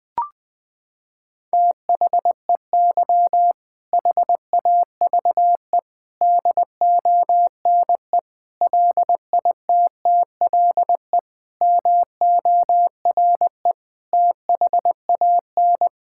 Morse code, heard as a keyed tone, about 20 wpm, 700 Hz. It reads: THEY HAVE DONE LITTLE MORE THAN
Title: Sentences from Top 1000 Words - Review Part 5 - 20wpm